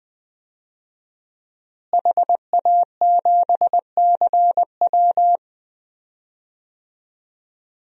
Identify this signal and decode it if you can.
Morse code, heard as a keyed tone, about 21 wpm, 700 Hz. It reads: HA7CW
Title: DXCC-100 Call Signs - 5 Characters - Repeated 3x - 20wpm